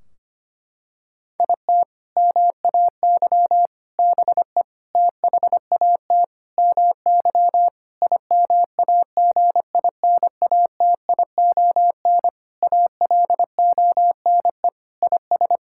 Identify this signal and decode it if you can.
Morse code, heard as a keyed tone, about 25 wpm, 700 Hz. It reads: IT MAY BE THAT MY IMAGINATION ALONE IS
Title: Sentences from Top 1600 Words 25wpm